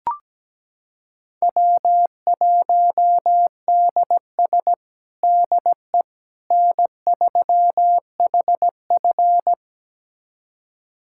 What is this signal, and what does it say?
Morse code, heard as a keyed tone, about 17 wpm, 700 Hz. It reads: W1DS DE N3HF
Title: USA Callsign DE Callsign - 4 Characters - 17wpm